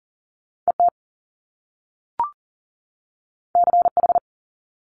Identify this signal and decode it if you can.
Morse code, heard as a keyed tone, about 38 wpm, 700 Hz. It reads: ET CH
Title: Top Two Letter Combinations in Words 40wpm